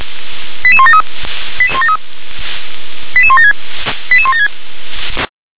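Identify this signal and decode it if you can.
Five-tone selective call (ZVEI), heard as quick runs of five tones, tones of about 70 ms. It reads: 88173 88173 88176 88176
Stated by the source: Five-tone ZVEI selective call 88177, received by SDR.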